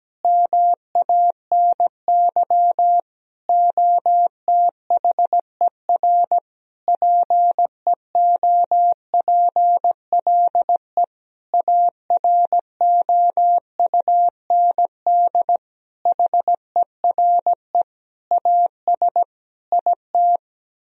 Morse code, heard as a keyed tone, about 17 wpm, 700 Hz. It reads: MANY OTHER PEOPLE AROUND HERE AS IT
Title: Sentences from Top 1000 Words - Review Part 3 - 17wpm